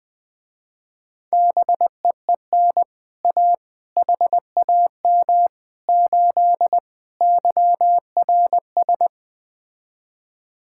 Morse code, heard as a keyed tone, about 20 wpm, 700 Hz. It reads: BEEN A HAM 8 YRS
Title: Speed-Racing Common QSO Phrases - 17wpm